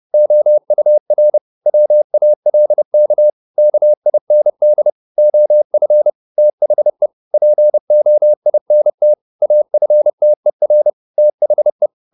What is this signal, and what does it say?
Morse code, heard as a keyed tone, about 30 wpm, 600 Hz. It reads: OUR WALK KIND OF THE POINT AFTER THE